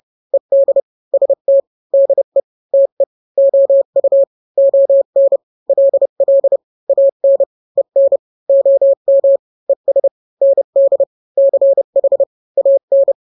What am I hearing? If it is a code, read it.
Morse code, heard as a keyed tone, about 30 wpm, 550 Hz. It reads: ED ST DE TE OU ON LL AN EN OM ES ND CH AN
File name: n-grams / 2-grams 30 25 WPM